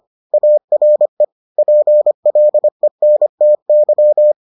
Morse code, heard as a keyed tone, about 25 wpm, 600 Hz. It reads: ARE PLENTY